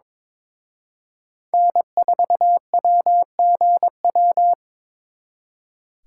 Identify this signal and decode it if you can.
Morse code, heard as a keyed tone, about 26 wpm, 700 Hz. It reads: N4WGW
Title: Speed Racing - USA 1x3 Call Signs 22wpm